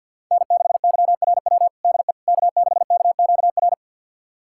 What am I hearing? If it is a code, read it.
Morse code, heard as a keyed tone, about 51 wpm, 700 Hz. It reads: N6YRW DE K6KQR
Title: USA Callsign DE Callsign - 5 Characters - 50wpm